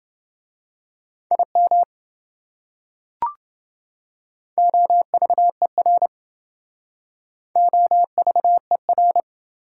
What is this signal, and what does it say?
Morse code, heard as a keyed tone, about 27 wpm, 700 Hz. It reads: IM OVER OVER